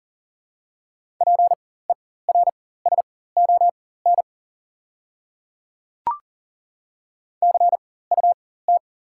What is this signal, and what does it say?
Morse code, heard as a keyed tone, about 40 wpm, spaced as for 19 wpm, 690 Hz. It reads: PERSON CUT
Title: 4x Character Spacing - Top 500 Words - 40wpm